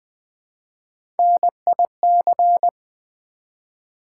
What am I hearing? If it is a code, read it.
Morse code, heard as a keyed tone, about 20 wpm, 700 Hz. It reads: NIC
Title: Build A Word - 1 Letter At A Time - 4-Letter Words - 20wpm